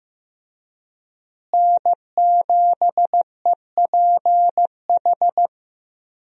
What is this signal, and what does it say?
Morse code, heard as a keyed tone, about 15 wpm, 700 Hz. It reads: N7EPH
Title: USA Call Signs - 5 Characters - Repeated 2x - 15wpm